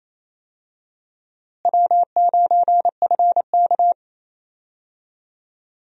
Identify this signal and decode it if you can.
Morse code, heard as a keyed tone, about 27 wpm, 700 Hz. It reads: W9FK
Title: USA Call Signs - 4 Characters - Repeated 2x - 28wpm